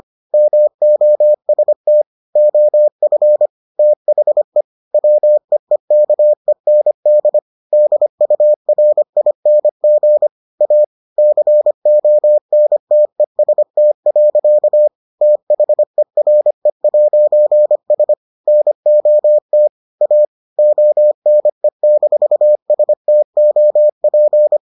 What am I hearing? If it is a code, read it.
Morse code, heard as a keyed tone, about 25 wpm, 600 Hz. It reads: MOST OF THE WEEKEND DURING A CONTEST. THERE'S NOT A ONE-STOP